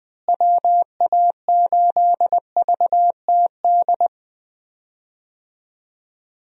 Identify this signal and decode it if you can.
Morse code, heard as a keyed tone, about 20 wpm, 700 Hz. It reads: WA8VTD
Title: USA Call Signs - 6 Characters - Repeated 2x - 20wpm